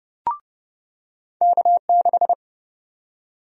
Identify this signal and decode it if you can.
Morse code, heard as a keyed tone, about 30 wpm, 700 Hz. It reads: K6
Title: USA Call Sign Prefixes - 30wpm